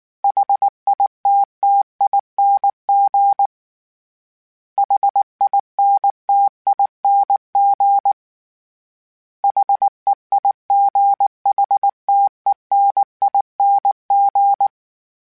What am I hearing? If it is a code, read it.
Morse code, heard as a keyed tone, about 19 wpm, 800 Hz. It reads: HITTING HINTING HEIGHTENING
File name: Special INGs / Special INGs-mp3s / Sngl Ing @18